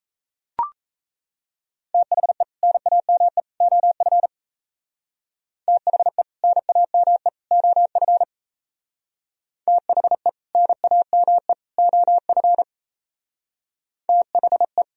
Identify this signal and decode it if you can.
Morse code, heard as a keyed tone, about 33 wpm, 700 Hz. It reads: THE NAME OF THE NAME OF THE NAME OF THE
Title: Speed-Racing - Sets of 3 Words 28wpm